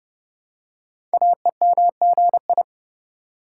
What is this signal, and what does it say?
Morse code, heard as a keyed tone, about 29 wpm, 700 Hz. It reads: AEMGI
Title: Code Groups - 5 Letters - 30wpm